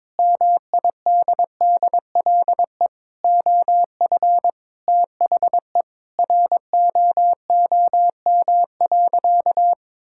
Morse code, heard as a keyed tone, about 22 wpm, 700 Hz. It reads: MIDDLE OF THE ROOM.
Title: Sentences from Top 1000 Words - Review Part 6 - 22wpm